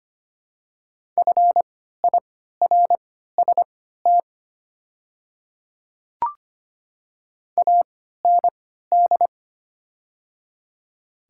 Morse code, heard as a keyed tone, about 25 wpm, spaced as for 14 wpm, 700 Hz. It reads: FIRST AND
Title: 3x Character Spacing - Top 100 Words - 25wpm